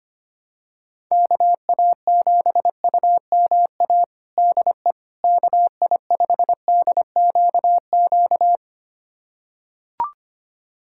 Morse code, heard as a keyed tone, about 25 wpm, 700 Hz. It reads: KA7UMA DE KI5DQQ
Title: USA Callsign DE Callsign - 6 Characters - 25wpm